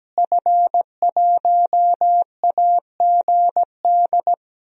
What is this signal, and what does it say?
Morse code, heard as a keyed tone, about 17 wpm, 700 Hz. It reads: F1AGD
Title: DXCC-100 Call Signs - 5 Characters - Repeated 3x - 17wpm